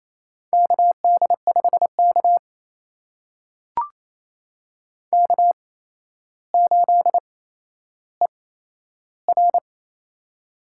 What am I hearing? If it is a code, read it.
Morse code, heard as a keyed tone, about 27 wpm, spaced as for 12 wpm, 700 Hz. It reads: KD5K K8ER
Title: USA Call Signs - 4 Characters - 8x Character Spacing - 28wpm